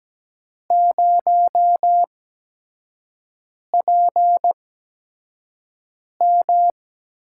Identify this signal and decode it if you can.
Morse code, heard as a keyed tone, about 17 wpm, spaced as for 5 wpm, 700 Hz. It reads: 0PM
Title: USA Call Signs - 4 Characters - 8x Character Spacing - 17wpm